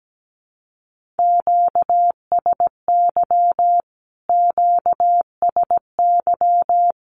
Morse code, heard as a keyed tone, about 17 wpm, 700 Hz. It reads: QSY QSY
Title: Q-Codes Repeated 3x 17wpm